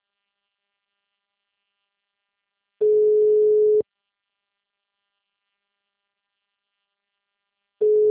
Ringback tone (Europe, 425 Hz).
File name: out- -2028- - - .86